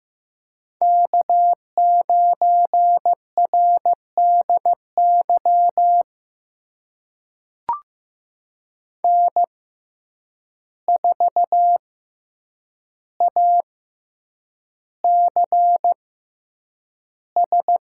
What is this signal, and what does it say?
Morse code, heard as a keyed tone, about 15 wpm, spaced as for 8 wpm, 700 Hz. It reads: K9RDY N4ACS
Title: USA Call Signs - 5 Characters - 6x Character Spacing - 15wpm